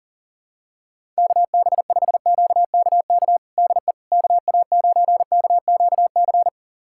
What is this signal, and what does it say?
Morse code, heard as a keyed tone, about 37 wpm, 700 Hz. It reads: KB5QKK DE KA9KQC
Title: USA Callsign DE Callsign - 6 Characters - 40wpm